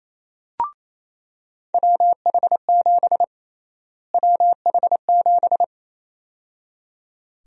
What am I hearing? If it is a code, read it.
Morse code, heard as a keyed tone, about 28 wpm, 700 Hz. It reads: WH7 WH7
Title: USA Call Sign Prefixes - Repeated 2x - 28wpm